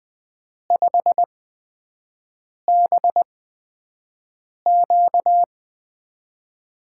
Morse code, heard as a keyed tone, about 20 wpm, spaced as for 5 wpm, 700 Hz. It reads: 5BQ
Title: USA Call Signs - 5 Characters - 8x Character Spacing - 20wpm